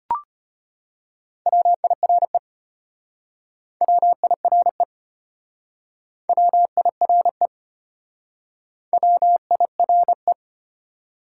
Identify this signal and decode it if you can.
Morse code, heard as a keyed tone, about 32 wpm, 700 Hz. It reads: WIRE WIRE WIRE WIRE